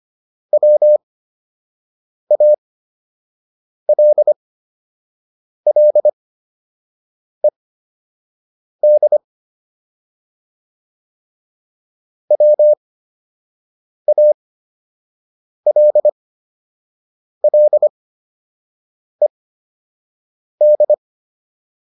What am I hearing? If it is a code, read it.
Morse code, heard as a keyed tone, about 25 wpm, spaced as for 6 wpm, 600 Hz. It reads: WALLED WALLED